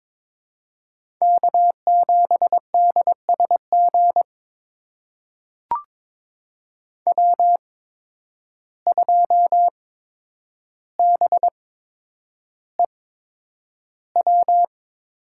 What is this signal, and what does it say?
Morse code, heard as a keyed tone, about 22 wpm, spaced as for 9 wpm, 700 Hz. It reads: K7DSG W2BEW
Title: USA Call Signs - 5 Characters - 8x Character Spacing - 22wpm